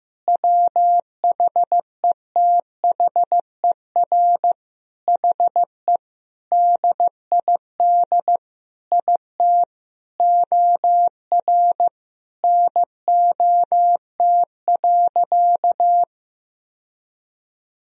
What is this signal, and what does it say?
Morse code, heard as a keyed tone, about 15 wpm, 700 Hz. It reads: WHETHER HE DID IT OR NOT.
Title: Sentences from Top 1000 Words - Review Part 3 - 15wpm